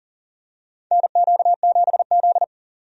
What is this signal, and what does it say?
Morse code, heard as a keyed tone, about 40 wpm, 700 Hz. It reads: NQ7Z
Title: USA Call Signs - 4 Characters - 40wpm